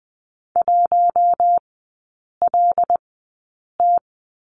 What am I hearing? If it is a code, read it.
Morse code, heard as a keyed tone, about 20 wpm, spaced as for 8 wpm, 700 Hz. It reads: 1LT
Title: 121 Lesson 20e - Number 3 - Three Characters 20wpm